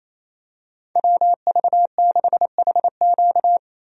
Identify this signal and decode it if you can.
Morse code, heard as a keyed tone, about 28 wpm, 700 Hz. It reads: WV6HQ
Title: USA Call Signs - 5 Characters - 28wpm